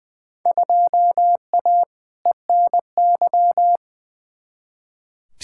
Morse code, heard as a keyed tone, about 20 wpm, 700 Hz. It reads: 2A ENY
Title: Field Day Exchanges Encore - 20wpm